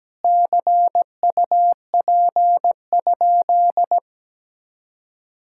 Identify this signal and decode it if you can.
Morse code, heard as a keyed tone, about 17 wpm, 700 Hz. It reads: CUP?
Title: Sentences from Top 1600 Words 17wpm